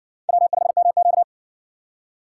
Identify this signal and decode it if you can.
Morse code, heard as a keyed tone, about 54 wpm, 700 Hz. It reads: W5MQ